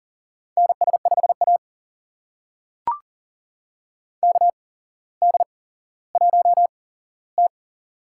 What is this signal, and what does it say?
Morse code, heard as a keyed tone, about 39 wpm, spaced as for 17 wpm, 700 Hz. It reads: NS5A KD1T
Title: USA Call Signs - 4 Characters - 8x Character Spacing - 40wpm